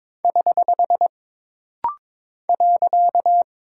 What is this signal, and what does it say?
Morse code, heard as a keyed tone, about 22 wpm, 700 Hz. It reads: <HH> .
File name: practice / 22wpm / Single Punctuation-Prosign - Rapid-Fire - 22wpm